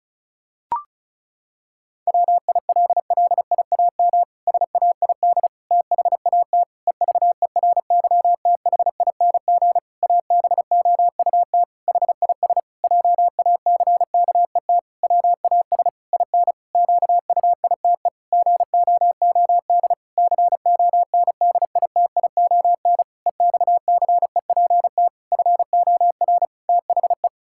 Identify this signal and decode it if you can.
Morse code, heard as a keyed tone, about 34 wpm, 700 Hz. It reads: WILLIAM SAID THAT EVERYTHING ABOUT HIS JACKET WAS IN QUITE GOOD CONDITION EXCEPT FOR THE
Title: Pangram Challenge v2 - 35wpm